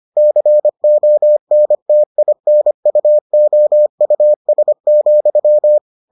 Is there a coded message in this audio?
Yes: Morse code from 0.2 to 5.8 s, heard as a keyed tone, about 25 wpm, 600 Hz. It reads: CONTINUOUS,